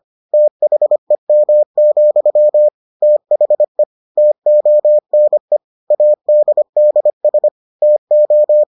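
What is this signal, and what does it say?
Morse code, heard as a keyed tone, about 25 wpm, 600 Hz. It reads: THEM, THE TONE ADDS TO
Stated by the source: Morse edition of the Foundations of Amateur Radio episodes.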